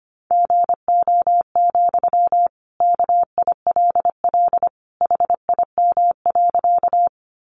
Morse code, heard as a keyed tone, about 25 wpm, 700 Hz. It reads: GO, KILL HIM.
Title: Sentences from Top 1400 Words 25wpm